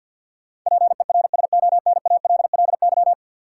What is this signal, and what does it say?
Morse code, heard as a keyed tone, about 51 wpm, 700 Hz. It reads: PERSONALLY
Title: Sentences from Top 1400 Words - New Word Drill Alt - 50wpm